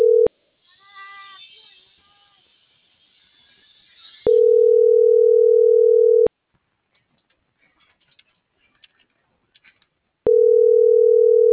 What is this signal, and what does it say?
Ringback tone (North America), 440 + 480 Hz.
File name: CENTRAL MABE / out antonio.garcia - .97694